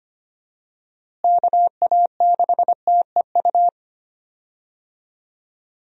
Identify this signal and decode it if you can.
Morse code, heard as a keyed tone, about 25 wpm, 700 Hz. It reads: KA6TEU
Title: USA Call Signs - 6 Characters - Repeated 2x - 25wpm